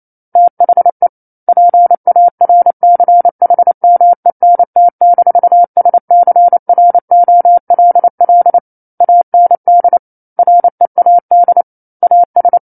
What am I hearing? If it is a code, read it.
Morse code, heard as a keyed tone, about 28 wpm, 700 Hz. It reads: THE PARCHMENT-SCROLL AND READ AS
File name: 09-Chapter-09 28 WPM